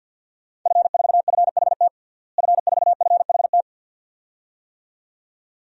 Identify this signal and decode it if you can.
Morse code, heard as a keyed tone, about 53 wpm, 700 Hz. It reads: F4FHT F4FHT
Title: DXCC-100 Call Signs - 5 Characters - Repeated 2x - 50wpm